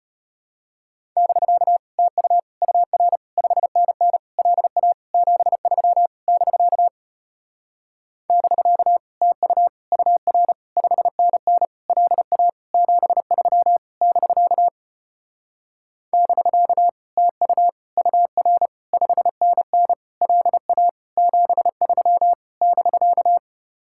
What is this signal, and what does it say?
Morse code, heard as a keyed tone, about 33 wpm, 700 Hz. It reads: <BK> TU UR 5NN LA 73 <BK> <BK> TU UR 5NN LA 73 <BK> <BK> TU UR 5NN LA 73 <BK>